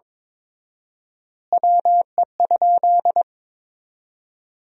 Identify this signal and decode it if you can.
Morse code, heard as a keyed tone, about 22 wpm, 700 Hz. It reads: WE?
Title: 4x Word-Spacing - Sentences from Top 200 Words - 22wpm